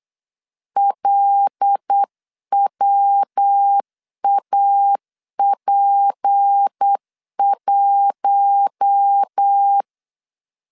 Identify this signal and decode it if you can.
Morse code, heard as a keyed tone, about 8 wpm, 800 Hz. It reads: LWAP1
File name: CW / DET / 1 9 1500C S C 8wpm 5653